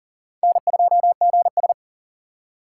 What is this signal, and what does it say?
Morse code, heard as a keyed tone, about 40 wpm, 700 Hz. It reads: N2GS
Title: USA Call Signs - 3x Character Spacing - 40wpm